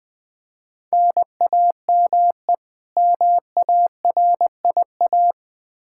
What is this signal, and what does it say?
Morse code, heard as a keyed tone, about 21 wpm, 700 Hz. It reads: NAME MARIA
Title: Practice QSO - Exchange 1 - Beginning - 20wpm